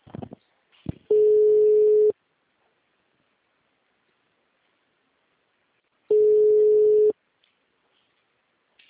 Ringback tone (Europe, 425 Hz).